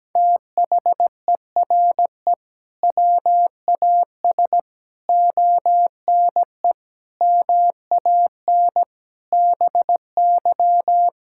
Morse code, heard as a keyed tone, about 17 wpm, 700 Hz. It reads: THERE WAS ONE MAN BY
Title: Sentences from Top 1600 Words 17wpm